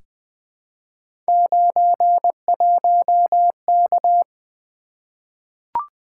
Morse code, heard as a keyed tone, about 20 wpm, 700 Hz. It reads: 91K
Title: Three Letter-Number Combinations - 20wpm